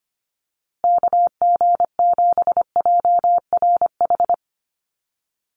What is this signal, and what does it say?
Morse code, heard as a keyed tone, about 25 wpm, 700 Hz. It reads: KG7JRH
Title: USA Call Signs Repeated 2x 25wpm